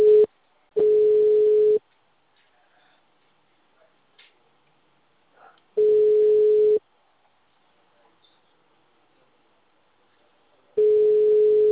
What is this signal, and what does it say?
Ringback tone (Europe, 425 Hz).
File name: out- -2050- - - .30689